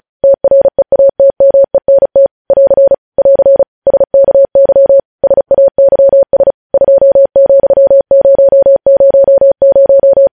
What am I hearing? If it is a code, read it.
Morse code, heard as a keyed tone, about 36 wpm, 550 Hz. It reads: TREATMENT <AR> <AR> SKY SAYS 2,000
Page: News headlines in Morse code at 35 wpm